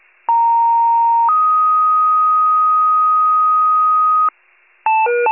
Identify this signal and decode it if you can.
Two-tone page, heard as a one-second tone, then a three-second tone, about 930 then 1250 Hz. It is heard from 0.3 to 4.3 s.
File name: pager / 39 RESCUE / wcalert